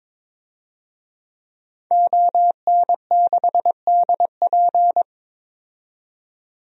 Morse code, heard as a keyed tone, about 22 wpm, 700 Hz. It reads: ON6DP
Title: DXCC-100 Call Signs - 5 Characters - 22wpm